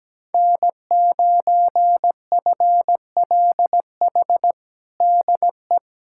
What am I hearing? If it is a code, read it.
Morse code, heard as a keyed tone, about 17 wpm, 700 Hz. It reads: N9FLH DE
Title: USA Callsign DE Callsign 17wpm